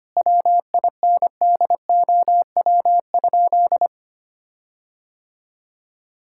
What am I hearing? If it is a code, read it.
Morse code, heard as a keyed tone, about 25 wpm, 700 Hz. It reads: WINDOW?
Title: Sentences from Top 1600 Words 25wpm